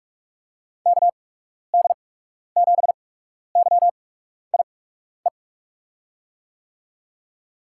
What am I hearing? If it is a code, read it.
Morse code, heard as a keyed tone, about 42 wpm, spaced as for 12 wpm, 700 Hz. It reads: KD7YIE
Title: USA Call Signs - 6 Characters - 8x Character Spacing - 45wpm